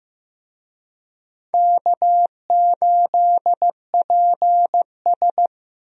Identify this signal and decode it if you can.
Morse code, heard as a keyed tone, about 16 wpm, 700 Hz. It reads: K8PS
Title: USA Call Signs - 4 Characters - Repeated 3x - 15wpm